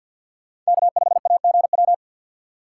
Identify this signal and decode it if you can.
Morse code, heard as a keyed tone, about 61 wpm, 700 Hz. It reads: K5AGW
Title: Speed Racing - USA Call Signs 50wpm